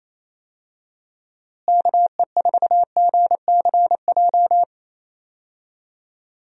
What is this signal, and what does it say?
Morse code, heard as a keyed tone, about 27 wpm, 700 Hz. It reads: KE4GCJ
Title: USA Call Signs - 6 Characters - Repeated 3x - 28wpm